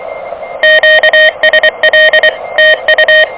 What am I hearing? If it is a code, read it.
Morse code, heard as a keyed tone, about 23 wpm, 590 Hz, with noise. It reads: QSL TU
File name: OK QRP 2023 / OK QRP 2023